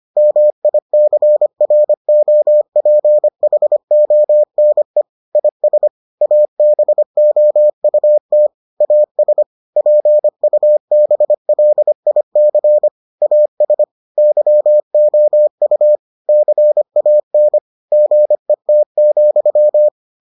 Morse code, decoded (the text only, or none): MICROPHONE IS ABOUT AS PUBLIC AS YOU CAN GET,